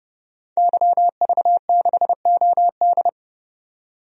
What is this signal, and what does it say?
Morse code, heard as a keyed tone, about 30 wpm, 700 Hz. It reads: YV6OD
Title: DXCC-100 Call Signs - 5 Characters - 30wpm